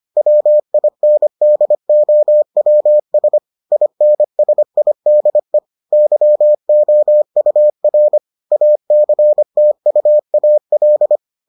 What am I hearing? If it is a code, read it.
Morse code, heard as a keyed tone, about 25 wpm, 600 Hz. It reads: WINDOWS INSIDE YOUR ACTUAL